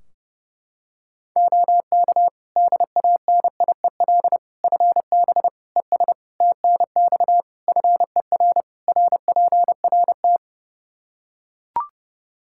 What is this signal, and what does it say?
Morse code, heard as a keyed tone, about 30 wpm, 700 Hz. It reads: OK DANIEL FB ES TNX FER RPRT
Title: K1USN SST QSO Practice 30wpm